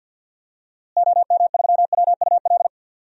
Morse code, heard as a keyed tone, about 48 wpm, 700 Hz. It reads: OM3WUL
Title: DXCC-100 Call Signs - 6 Characters - Repeated 3x - 50wpm